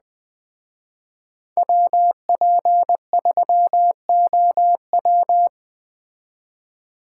Morse code, heard as a keyed tone, about 20 wpm, 700 Hz. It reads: WP3OW
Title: Speed Racing - USA 2x2 Call Signs 17wpm